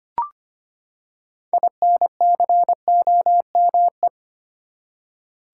Morse code, heard as a keyed tone, about 25 wpm, 700 Hz. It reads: INCOME
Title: Six Letter Words - 25wpm